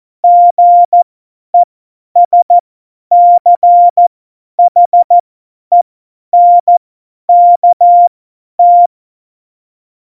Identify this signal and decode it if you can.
Morse code, heard as a keyed tone, about 14 wpm, spaced as for 10 wpm, 700 Hz. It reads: GESCHENKT
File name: Morsekurs / Witze / Witz 21 April 2020, 50 F70